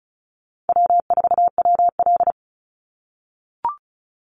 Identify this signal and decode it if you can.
Morse code, heard as a keyed tone, about 34 wpm, 700 Hz. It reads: W4WL
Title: USA Call Signs 35wpm